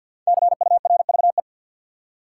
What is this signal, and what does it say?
Morse code, heard as a keyed tone, about 50 wpm, 700 Hz. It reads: CURVE
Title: Five Letter Words - 50wpm